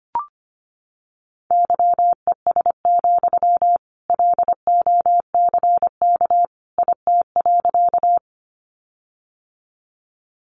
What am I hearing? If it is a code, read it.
Morse code, heard as a keyed tone, about 25 wpm, 700 Hz. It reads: YES, LOCK IT.
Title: Sentences from Top 1300 Words 25wpm